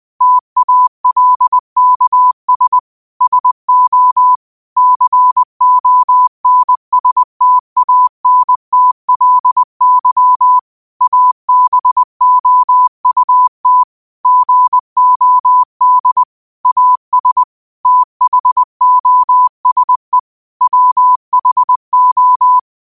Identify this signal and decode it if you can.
Morse code, heard as a keyed tone, about 20 wpm, 1000 Hz. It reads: TALKS SO CONSTANTLY ABOUT GOD AS THOSE WHO
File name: qotdfiles / 20wpm